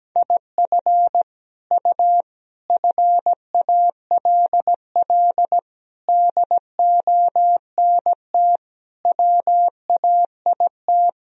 Morse code, heard as a keyed tone, about 17 wpm, 700 Hz. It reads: IF U FALL DONT WAIT
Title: Sentences from Top 500 Words and Abbreviations 17wpm